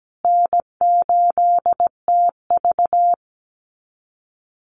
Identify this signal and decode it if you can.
Morse code, heard as a keyed tone, about 17 wpm, 700 Hz. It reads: N8TV
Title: USA Call Signs Repeated 3x 17wpm